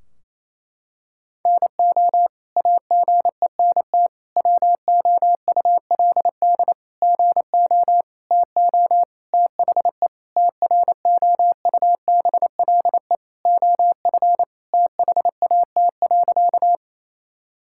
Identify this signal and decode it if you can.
Morse code, heard as a keyed tone, about 28 wpm, 700 Hz. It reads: NO AGENT WOULD GO TO THE TROUBLE OF THAT.
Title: Sentences from Top 1600 Words 28wpm